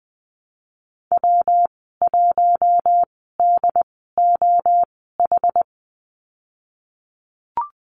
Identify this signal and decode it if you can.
Morse code, heard as a keyed tone, about 20 wpm, spaced as for 15 wpm, 700 Hz. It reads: W1DOH
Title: 167 Lesson 27g - Number 7 - Call Signs 20wpm